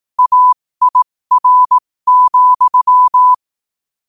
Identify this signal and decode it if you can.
Morse code, heard as a keyed tone, about 18 wpm, spaced as for 15 wpm, 1000 Hz. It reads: AIR,
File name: qotdfiles / 15wpm